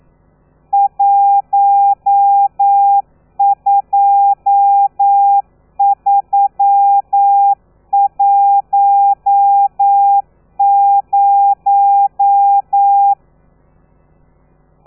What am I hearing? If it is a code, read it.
Morse code, heard as a keyed tone, about 9 wpm, 800 Hz. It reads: 12310
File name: telegrafia / i2tfj / Lez13 40